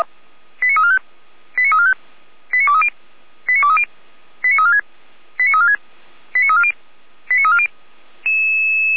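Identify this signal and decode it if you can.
Five-tone selective call (ZVEI), heard as quick runs of five tones, tones of about 70 ms. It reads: 89345 89345 89230 89230 89346 89346 89340 89340
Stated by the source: ZVEI five-tone selective call received via SDR.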